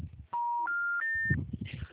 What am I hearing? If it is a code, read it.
Special information tone, with noise.